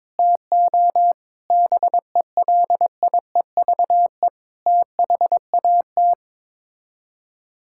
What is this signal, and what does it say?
Morse code, heard as a keyed tone, about 22 wpm, 700 Hz. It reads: TO BELIEVE THAT
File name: practice / 22wpm / Sets of 7 Words - 22wpm